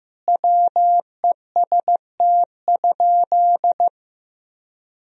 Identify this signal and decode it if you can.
Morse code, heard as a keyed tone, about 16 wpm, 700 Hz. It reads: WEST?
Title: Sentences from Top 1000 Words - Review Part 2 - 15wpm